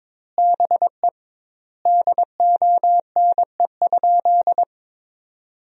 Morse code, heard as a keyed tone, about 22 wpm, 700 Hz. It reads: BE DONE?
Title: 2x Word-Spacing - Sentences from Top 500 Words - 22wpm